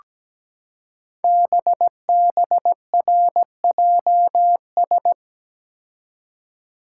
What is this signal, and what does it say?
Morse code, heard as a keyed tone, about 17 wpm, 700 Hz. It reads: BBRJS
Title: Code Groups - 5 Letters - 17wpm